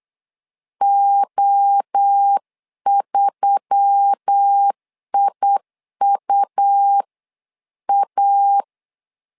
Morse code, heard as a keyed tone, about 8 wpm, 800 Hz. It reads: O3IUA
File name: LEZIONI / 4 Q1S3 8wpm 3037